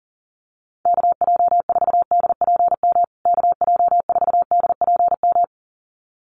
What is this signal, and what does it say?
Morse code, heard as a keyed tone, about 40 wpm, 700 Hz. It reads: KJ4DPM KJ4DPM
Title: USA Call Signs Repeated 2x 40wpm